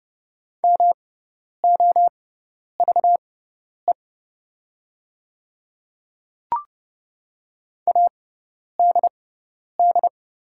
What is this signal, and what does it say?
Morse code, heard as a keyed tone, about 30 wpm, spaced as for 10 wpm, 700 Hz. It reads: MOVE ADD
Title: 6x Character Spacing - Top 200 Words - 30wpm